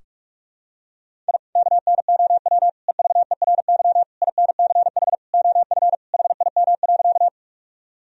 Morse code, heard as a keyed tone, about 46 wpm, 700 Hz. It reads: I KNOW EVERY INCH OF HIM.
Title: Sentences from Top 1000 Words - Review Part 3 - 45wpm